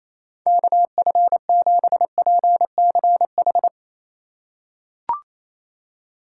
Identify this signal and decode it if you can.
Morse code, heard as a keyed tone, about 28 wpm, 700 Hz. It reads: KF7PCH
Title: USA Call Signs - 6 Characters - 2x Character Spacing - 28wpm